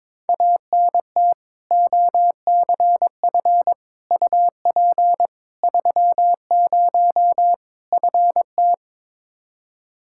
Morse code, heard as a keyed tone, about 22 wpm, 700 Hz. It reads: ANT OCF UP 30 FT
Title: Practice QSO - Exchange 2 - Info - 22wpm